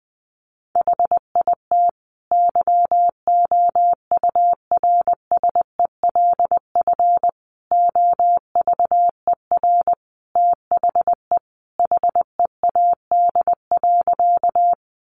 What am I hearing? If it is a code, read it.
Morse code, heard as a keyed tone, about 20 wpm, 700 Hz. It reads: HIT YOURSELF OVER THE HEAD.
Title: Sentences from Top 1200 Words 20wpm